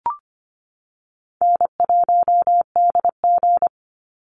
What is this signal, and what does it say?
Morse code, heard as a keyed tone, about 25 wpm, 700 Hz. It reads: N1DG
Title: Call Signs Repeated 2x - 25wpm